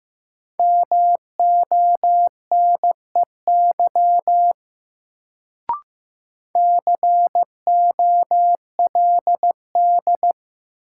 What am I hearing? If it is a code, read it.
Morse code, heard as a keyed tone, about 15 wpm, 700 Hz. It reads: MONEY COLD
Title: Sentences from Top 400 Words - New Word Drill - 15wpm